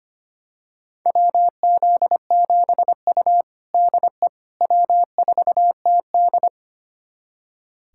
Morse code, heard as a keyed tone, about 25 wpm, 700 Hz. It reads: WZ7U DE W4TD
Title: USA Callsign DE Callsign - 4 Characters - 25wpm